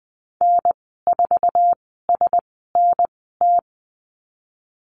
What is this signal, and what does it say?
Morse code, heard as a keyed tone, about 20 wpm, spaced as for 15 wpm, 700 Hz. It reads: N4SNT
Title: 048 Lesson 9g - Number 4 - Call Signs 20wpm